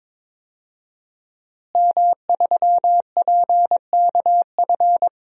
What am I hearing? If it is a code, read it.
Morse code, heard as a keyed tone, about 22 wpm, 700 Hz. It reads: M3PKF
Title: DXCC-100 Call Signs - 5 Characters - Repeated 3x - 22wpm